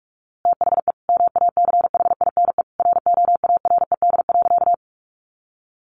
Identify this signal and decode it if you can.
Morse code, heard as a keyed tone, about 45 wpm, 700 Hz. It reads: THE MACHINE ROARED.
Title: A Sound Of Thunder 45wpm